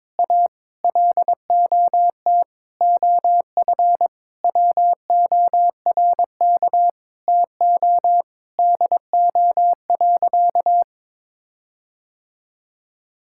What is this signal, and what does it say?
Morse code, heard as a keyed tone, about 22 wpm, 700 Hz. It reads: A LOT OF WORK TO DO.
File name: practice / 22wpm / Sentences with 8-Word Phrases - 22wpm